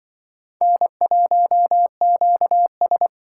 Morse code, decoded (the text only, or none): N1QS